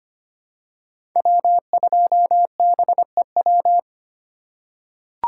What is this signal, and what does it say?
Morse code, heard as a keyed tone, about 25 wpm, 700 Hz. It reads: W2BEW
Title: USA Call Signs - 5 Characters - 2x Character Spacing - 25wpm